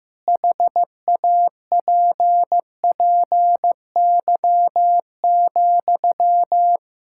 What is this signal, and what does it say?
Morse code, heard as a keyed tone, about 15 wpm, 700 Hz. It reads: HAPPY,
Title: Sentences from Top 1000 Words - Review Part 6 - 15wpm